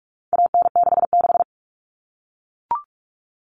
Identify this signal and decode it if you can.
Morse code, heard as a keyed tone, about 44 wpm, 700 Hz. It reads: AN66
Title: All Cut Numbers - 4-Digit 45wpm